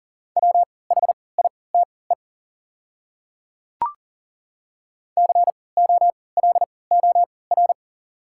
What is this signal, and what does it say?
Morse code, heard as a keyed tone, about 40 wpm, spaced as for 23 wpm, 700 Hz. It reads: WHITE COLOR
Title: 3x Character Spacing - Top 500 Words - 40wpm